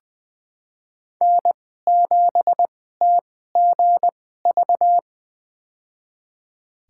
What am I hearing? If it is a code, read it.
Morse code, heard as a keyed tone, about 20 wpm, spaced as for 14 wpm, 700 Hz. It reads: N7TGV
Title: USA Call Signs - 2x Character Spacing - 20wpm